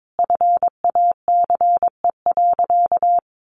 Morse code, heard as a keyed tone, about 22 wpm, 700 Hz. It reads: FACE.
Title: Sentences from Top 1100 Words 22wpm